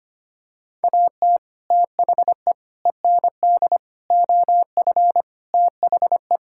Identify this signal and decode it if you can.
Morse code, heard as a keyed tone, about 25 wpm, 700 Hz. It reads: AT THE END OF THE
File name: practice / 25wpm / Sentences with 4-Word Phrases - 25wpm